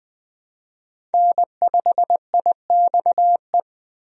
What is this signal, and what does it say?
Morse code, heard as a keyed tone, about 20 wpm, 700 Hz. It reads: N5IXE
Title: USA Call Signs - 5 Characters - Repeated 3x - 20wpm